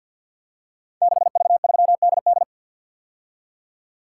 Morse code, heard as a keyed tone, about 51 wpm, 700 Hz. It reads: 6V3DD
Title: Code Groups - 5 Letter:Numbers - 50wpm